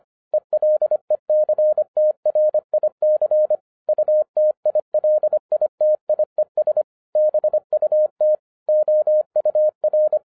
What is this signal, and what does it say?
Morse code, heard as a keyed tone, about 25 wpm, 610 Hz. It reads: ELECTRIC UTILITIES BUT OUR
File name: 2011-07-mp3 / n39 THE 3.11 2 25wpm